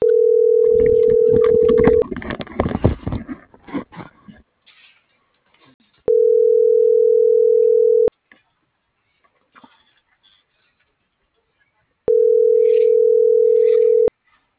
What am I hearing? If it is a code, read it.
Ringback tone (North America), 440 + 480 Hz.